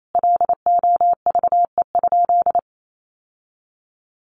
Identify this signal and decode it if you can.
Morse code, heard as a keyed tone, about 28 wpm, 700 Hz. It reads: LOVE?
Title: Sentences from Top 1100 Words 28wpm